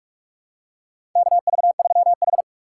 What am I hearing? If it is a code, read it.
Morse code, heard as a keyed tone, about 46 wpm, 700 Hz. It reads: KV3H
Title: USA Call Signs - 2x Character Spacing - 45wpm